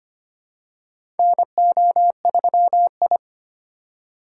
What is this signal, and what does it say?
Morse code, heard as a keyed tone, about 25 wpm, 700 Hz. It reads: NO3I
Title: USA Call Signs - 4 Characters - 4x Character Spacing - 25wpm